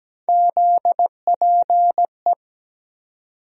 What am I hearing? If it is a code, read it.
Morse code, heard as a keyed tone, about 17 wpm, 700 Hz. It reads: ZPE
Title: USA Call Sign Postfixes - 3 Letters 17wpm